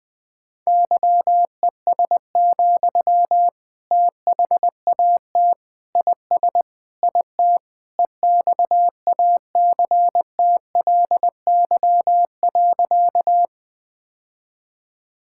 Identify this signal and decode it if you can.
Morse code, heard as a keyed tone, about 20 wpm, 700 Hz. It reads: YES, THAT IS IT EXACTLY.
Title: Sentences from Top 1000 Words - Review Part 6 - 20wpm